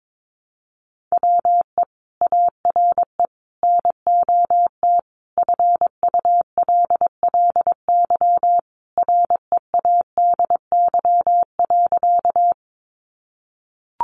Morse code, heard as a keyed tone, about 22 wpm, 700 Hz. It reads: WE ARE NOT FULLY READY.
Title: Sentences from Top 1300 Words 22wpm